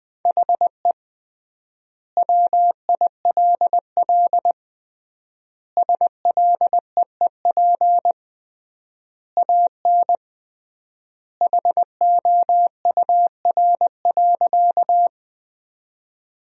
Morse code, heard as a keyed tone, about 21 wpm, 700 Hz. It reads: HE WILL SLEEP AN HOUR.
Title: 3x Word-Spacing - Sentences from Top 500 Words - 20wpm